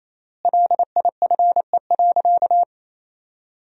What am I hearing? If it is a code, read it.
Morse code, heard as a keyed tone, about 28 wpm, 700 Hz. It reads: LIFE.
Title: 3x Word-Spacing - Sentences from Top 300 Words - 28wpm